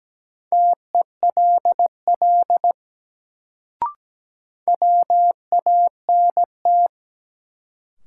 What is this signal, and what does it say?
Morse code, heard as a keyed tone, about 17 wpm, 700 Hz. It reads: TELL WANT
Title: Sets of 2 Words - New Word Drill - 17wpm